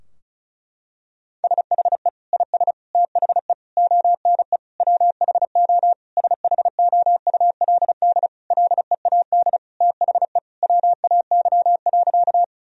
Morse code, heard as a keyed tone, about 34 wpm, 700 Hz. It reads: SHE IS THE ONE WHO SHOULD LEAD THE WAY.
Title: Sentences from Top 1000 Words - Review Part 2 - 35wpm